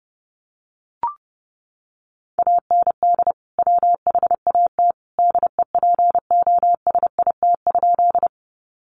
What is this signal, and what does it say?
Morse code, heard as a keyed tone, about 29 wpm, 700 Hz. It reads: AND WHAT DEPOSIT?
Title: Sentences from Top 1300 Words 30wpm